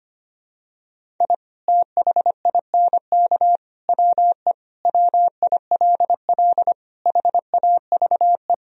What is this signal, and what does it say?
Morse code, heard as a keyed tone, about 25 wpm, 700 Hz. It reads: I THINK WE WILL HAVE
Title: Sentences from Top 1000 Words - Review Part 1 - 25wpm